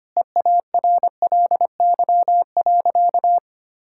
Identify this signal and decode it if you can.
Morse code, heard as a keyed tone, about 25 wpm, 700 Hz. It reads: EARLY.
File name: practice / 25wpm / Sentences with 9-Word Phrases - 25wpm